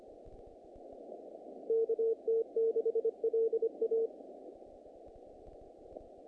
Morse code, heard as a keyed tone, about 25 wpm, 450 Hz. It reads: KT6LA
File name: contests / naqp-cw-k5tr-op-k5pi- -2100